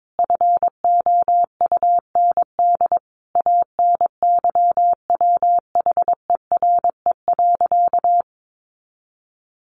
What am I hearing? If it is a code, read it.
Morse code, heard as a keyed tone, about 22 wpm, 700 Hz. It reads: FOUND ANYWHERE.
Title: Sentences from Top 1400 Words 22wpm